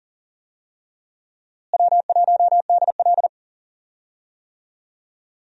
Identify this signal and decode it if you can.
Morse code, heard as a keyed tone, about 40 wpm, 700 Hz. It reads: W1DL
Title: USA Call Signs - 4 Characters - Repeated 2x - 40wpm